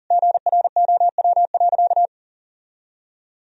Morse code, heard as a keyed tone, about 40 wpm, 700 Hz. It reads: GROW.
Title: Sentences from Top 1000 Words - Review Part 1 - 40wpm